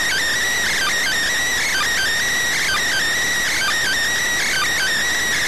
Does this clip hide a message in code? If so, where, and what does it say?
Yes: Emergency Alert System (SAME) tones (header bursts) from 0.6 to 4.9 s, heard as harsh buzzing bursts.